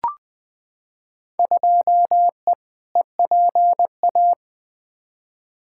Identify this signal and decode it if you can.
Morse code, heard as a keyed tone, about 20 wpm, 700 Hz. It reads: 2E EPA
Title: Field Day Exchanges Encore - 20wpm